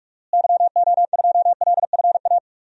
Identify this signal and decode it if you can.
Morse code, heard as a keyed tone, about 46 wpm, 700 Hz. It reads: YO2LFA
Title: DXCC-100 Call Signs - 6 Characters - Repeated 2x - 45wpm